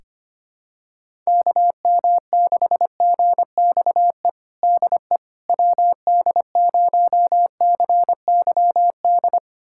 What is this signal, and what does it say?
Morse code, heard as a keyed tone, about 25 wpm, 700 Hz. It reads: KM6GXE DE WD0CYD
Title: USA Callsign DE Callsign - 6 Characters - 25wpm